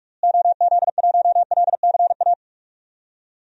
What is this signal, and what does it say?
Morse code, heard as a keyed tone, about 43 wpm, 700 Hz. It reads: OZ1LCA
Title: DXCC-100 Call Signs - 6 Characters - Repeated 2x - 45wpm